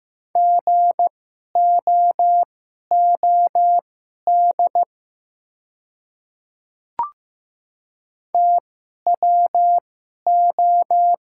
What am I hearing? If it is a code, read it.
Morse code, heard as a keyed tone, about 15 wpm, spaced as for 11 wpm, 700 Hz. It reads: GOOD TWO
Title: 2x Character Spacing - Top 500 Words - 15wpm